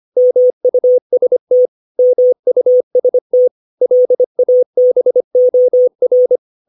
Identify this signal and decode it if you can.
Morse code, heard as a keyed tone, about 25 wpm, 500 Hz. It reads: MUST MUST LABOR
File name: 632 words / 632 Words 25WPM